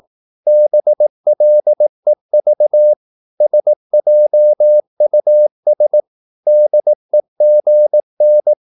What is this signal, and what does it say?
Morse code, heard as a keyed tone, about 18 wpm, 600 Hz. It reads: BLEV SJUS DEGN
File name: Korte ord DK 18wpm